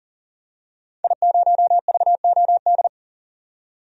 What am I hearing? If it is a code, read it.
Morse code, heard as a keyed tone, about 40 wpm, 700 Hz. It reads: I0VOD
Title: DXCC-100 Call Signs - 5 Characters - 40wpm